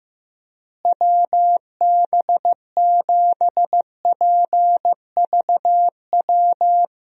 Morse code, heard as a keyed tone, about 15 wpm, 700 Hz. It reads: WB7PVW
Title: USA Call Signs - 6 Characters - Repeated 2x - 15wpm